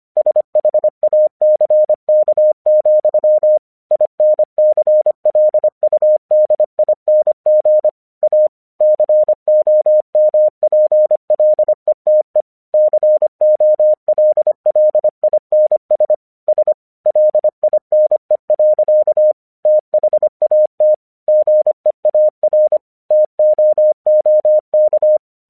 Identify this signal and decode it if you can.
Morse code, heard as a keyed tone, about 25 wpm, 610 Hz. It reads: SHACK, INCLUDING A COMPLETE COLLINS S LINE. THAT GEAR TOOK
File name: 1 English 25wpm / AQC12